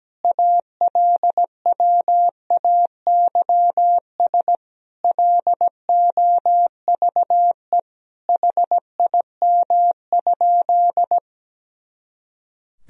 Morse code, heard as a keyed tone, about 17 wpm, 700 Hz. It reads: ALWAYS LOVE HIM?
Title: Sentences from Top 400 Words and Abbreviations 17wpm